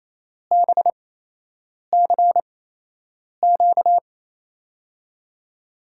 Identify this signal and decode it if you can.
Morse code, heard as a keyed tone, about 28 wpm, spaced as for 8 wpm, 700 Hz. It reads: BCQ
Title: USA Call Signs - 8x Character Spacing - 28wpm